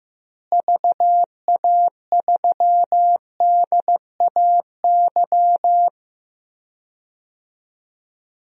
Morse code, heard as a keyed tone, about 15 wpm, 700 Hz. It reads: VA3DAY
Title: DXCC-100 Call Signs - 6 Characters - Repeated 3x - 15wpm